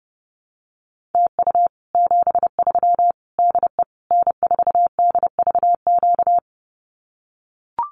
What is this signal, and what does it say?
Morse code, heard as a keyed tone, about 28 wpm, 700 Hz. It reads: TU 73 DE N4DVQ
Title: Common QSO Phrases Encore 30wpm